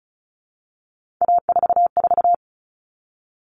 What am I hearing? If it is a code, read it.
Morse code, heard as a keyed tone, about 34 wpm, 700 Hz. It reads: A44
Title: All Cut Numbers - 3-Digit 35wpm